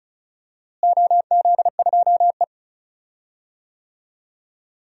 Morse code, heard as a keyed tone, about 36 wpm, 700 Hz. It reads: OZ2E
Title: DXCC-100 Call Signs - 4 Characters - Repeated 2x - 35wpm